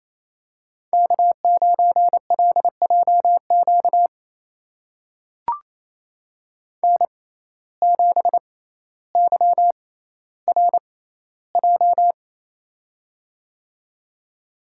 Morse code, heard as a keyed tone, about 28 wpm, spaced as for 14 wpm, 700 Hz. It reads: K9LJQ N7YRJ
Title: USA Call Signs - 5 Characters - 6x Character Spacing - 28wpm